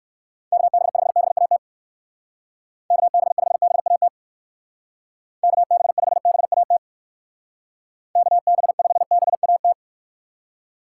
Morse code, heard as a keyed tone, about 56 wpm, 700 Hz. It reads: KB5BAT KB5BAT KB5BAT KB5BAT
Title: Speed Racing - USA Call Signs 45wpm